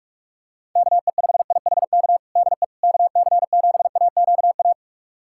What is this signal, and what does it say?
Morse code, heard as a keyed tone, about 46 wpm, 690 Hz. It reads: KE5IHK DE KC7AQA
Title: USA Callsign DE Callsign - 6 Characters - 45wpm